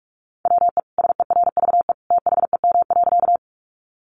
Morse code, heard as a keyed tone, about 44 wpm, 700 Hz. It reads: WE SERVE THEM.
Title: Sentences from Top 400 Words 45wpm